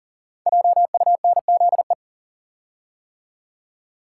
Morse code, heard as a keyed tone, about 42 wpm, 690 Hz. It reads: JUNZE
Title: Code Groups - 5 Letters - 40wpm